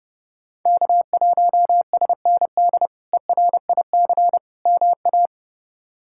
Morse code, heard as a keyed tone, about 30 wpm, 700 Hz. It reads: K1SND ERIC MA
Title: K1USN SST QSO Practice - 30wpm